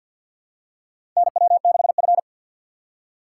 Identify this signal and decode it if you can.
Morse code, heard as a keyed tone, about 48 wpm, 700 Hz. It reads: NW6F
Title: USA Call Signs - 4 Characters - 3x Character Spacing - 50wpm